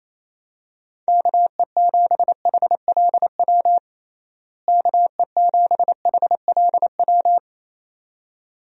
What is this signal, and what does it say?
Morse code, heard as a keyed tone, about 28 wpm, 700 Hz. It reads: KE7HLW KE7HLW
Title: USA Call Signs - 6 Characters - Repeated 2x - 28wpm